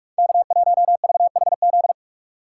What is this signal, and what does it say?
Morse code, heard as a keyed tone, about 41 wpm, 700 Hz. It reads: K1VHZ
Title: USA Call Signs - 5 Characters - 2x Character Spacing - 45wpm